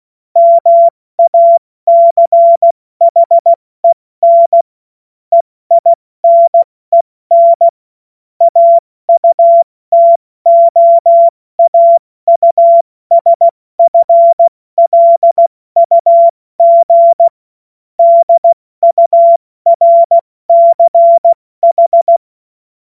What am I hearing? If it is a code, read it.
Morse code, heard as a keyed tone, about 16 wpm, 680 Hz. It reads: MACHEN EINEN AUTOAUSFLUG DURCH
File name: Morsekurs / Witze / Witz 14. Dezember 2021, 70 F80